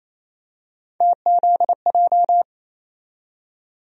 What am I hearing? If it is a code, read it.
Morse code, heard as a keyed tone, about 27 wpm, 700 Hz. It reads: TZJ
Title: USA Call Sign Postfixes - 3 Letters 28wpm